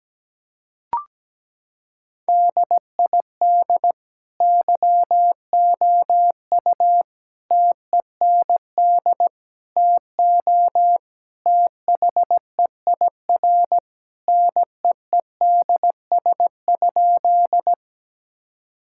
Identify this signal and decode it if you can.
Morse code, heard as a keyed tone, about 17 wpm, 700 Hz. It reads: DID YOU TEND TO THEIR NEEDS?
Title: Sentences from Top 1000 Words - Review Part 5 - 17wpm